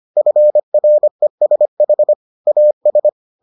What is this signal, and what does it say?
Morse code, heard as a keyed tone, about 25 wpm, 600 Hz. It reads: FRESH AS